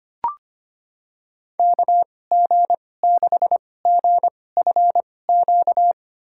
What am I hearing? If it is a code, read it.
Morse code, heard as a keyed tone, about 25 wpm, spaced as for 18 wpm, 700 Hz. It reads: KG6GFQ
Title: USA Call Signs - 6 Characters - 2x Character Spacing - 25wpm